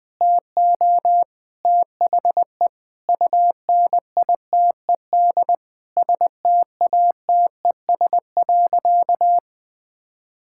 Morse code, heard as a keyed tone, about 21 wpm, 700 Hz. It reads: TO THE UNITED STATES.
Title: Sentences with 4-Word Phrases - 20wpm